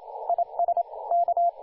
Morse code, heard as a keyed tone, about 31 wpm, spaced as for 27 wpm, 690 Hz, with radio noise. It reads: IS K
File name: du1-ja1pbv-1-28-040mhz-cw-cqing